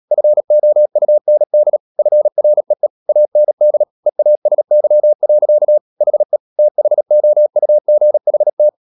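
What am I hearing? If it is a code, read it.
Morse code, heard as a keyed tone, about 37 wpm, 600 Hz. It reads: FOUND FREE AND EASY. HE THOUGHT
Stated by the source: Morse code practice at 35–39 wpm sending English text of Ken Follett's novel Never, from CW med Gnister.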